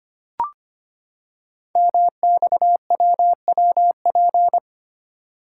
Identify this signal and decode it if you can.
Morse code, heard as a keyed tone, about 25 wpm, 700 Hz. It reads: MXWWP
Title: Code Groups - 5 Letters - 25wpm